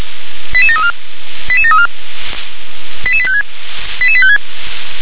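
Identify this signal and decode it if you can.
Five-tone selective call (ZVEI), heard as quick runs of five tones, tones of about 80 ms. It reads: 88734 88734 88856 88856
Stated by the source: ZVEI five-tone call received by SDR.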